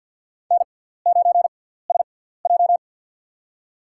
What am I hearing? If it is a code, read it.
Morse code, heard as a keyed tone, about 50 wpm, spaced as for 17 wpm, 700 Hz. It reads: N9SJ
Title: USA Call Signs - 6x Character Spacing - 50wpm